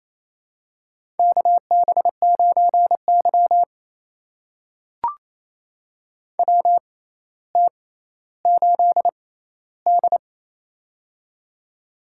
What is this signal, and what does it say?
Morse code, heard as a keyed tone, about 27 wpm, spaced as for 14 wpm, 700 Hz. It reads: KB9Y WT8D
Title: USA Call Signs - 4 Characters - 6x Character Spacing - 28wpm